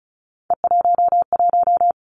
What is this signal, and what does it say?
Morse code, heard as a keyed tone, about 36 wpm, 700 Hz. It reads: E11
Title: Three Challenging Letter-Number Combinations 35wpm